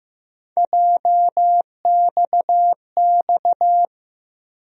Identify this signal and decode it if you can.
Morse code, heard as a keyed tone, about 15 wpm, 700 Hz. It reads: JXX
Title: USA Call Sign Postfixes - 3 Letters 15wpm